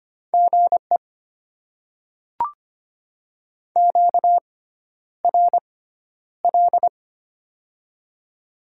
Morse code, heard as a keyed tone, about 25 wpm, spaced as for 11 wpm, 700 Hz. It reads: GE QRL
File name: Top 20 QSO Elements - 6x Character Spacing - 25wpm